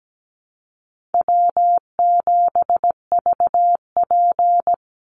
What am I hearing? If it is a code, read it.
Morse code, heard as a keyed tone, about 17 wpm, 700 Hz. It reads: W7VP
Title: Easy Call Signs Repeated 3x - 17wpm